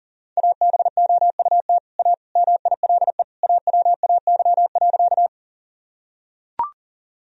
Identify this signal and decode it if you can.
Morse code, heard as a keyed tone, about 37 wpm, 700 Hz. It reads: ABOUT A MILE AWAY.
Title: Sentences from Top 1000 Words - Review Part 1 - 40wpm